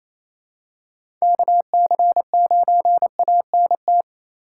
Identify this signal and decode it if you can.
Morse code, heard as a keyed tone, about 28 wpm, 700 Hz. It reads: KC9ANT
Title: USA Call Signs - 6 Characters - 4x Character Spacing - 28wpm